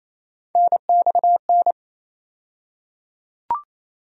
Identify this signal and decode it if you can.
Morse code, heard as a keyed tone, about 28 wpm, 700 Hz. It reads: NXN
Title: USA Call Sign Postfixes - 3 Letters 28wpm